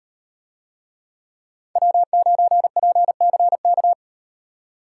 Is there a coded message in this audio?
Yes: Morse code from 1.8 to 3.9 s, heard as a keyed tone, about 38 wpm, 700 Hz. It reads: W9PCK